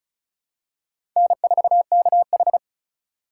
Morse code, heard as a keyed tone, about 34 wpm, 700 Hz. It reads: N4KH
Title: K1USN SST QSO Practice - 35wpm